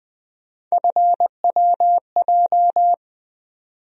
Morse code, heard as a keyed tone, about 20 wpm, 700 Hz. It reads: FWJ
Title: USA Call Sign Postfixes - 3 Letters 20wpm